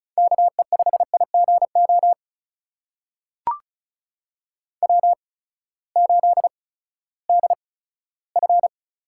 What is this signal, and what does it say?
Morse code, heard as a keyed tone, about 36 wpm, spaced as for 18 wpm, 700 Hz. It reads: KE5IGO W8DF
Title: USA Call Signs - 8x Character Spacing - 35wpm